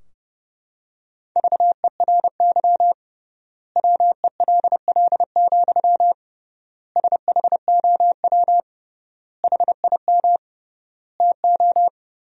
Morse code, heard as a keyed tone, about 32 wpm, 700 Hz. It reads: VERY WELL, SHOW HIM TO
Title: 3x Word-Spacing - Sentences from Top 200 Words - 30wpm